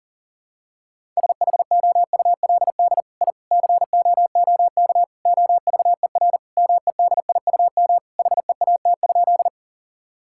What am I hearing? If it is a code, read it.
Morse code, heard as a keyed tone, about 37 wpm, 700 Hz. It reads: SHOULD I COOK OVER MEDIUM HEAT?
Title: Sentences from Top 1000 Words - Review Part 6 - 40wpm